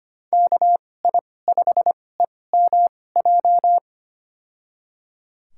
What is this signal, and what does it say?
Morse code, heard as a keyed tone, about 25 wpm, spaced as for 18 wpm, 700 Hz. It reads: KI5EMJ
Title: USA Call Signs - 6 Characters - 2x Character Spacing - 25wpm